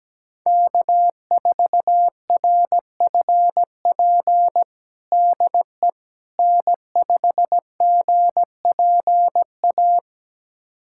Morse code, heard as a keyed tone, about 17 wpm, 700 Hz. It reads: K4RFP DE N5GPA
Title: USA Callsign DE Callsign - 5 Characters - 17wpm